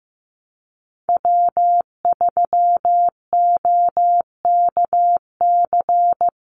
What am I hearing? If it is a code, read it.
Morse code, heard as a keyed tone, about 15 wpm, 700 Hz. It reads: W3OKC
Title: Call Signs Repeated 2x - 15wpm